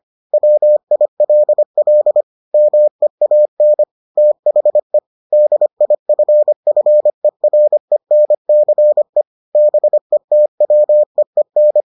Morse code, heard as a keyed tone, about 25 wpm, 600 Hz. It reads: WILL MEAN THE DIFFERENCE BETWEEN